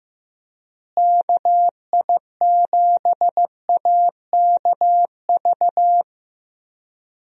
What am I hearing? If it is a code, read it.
Morse code, heard as a keyed tone, about 15 wpm, 700 Hz. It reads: KI7AKV
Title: USA Call Signs - 6 Characters - Repeated 2x - 15wpm